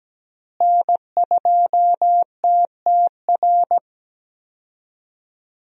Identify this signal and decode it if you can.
Morse code, heard as a keyed tone, about 17 wpm, 700 Hz. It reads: N2TTR
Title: USA Call Signs - 5 Characters - Repeated 2x - 17wpm